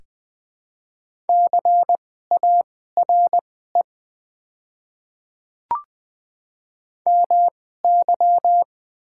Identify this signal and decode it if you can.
Morse code, heard as a keyed tone, about 20 wpm, spaced as for 14 wpm, 690 Hz. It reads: CARE MY
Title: 2x Character Spacing - Top 500 Words - 20wpm